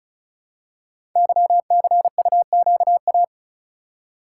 Morse code, heard as a keyed tone, about 36 wpm, 700 Hz. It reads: YCUQA
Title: Code Groups - 5 Letters - 35wpm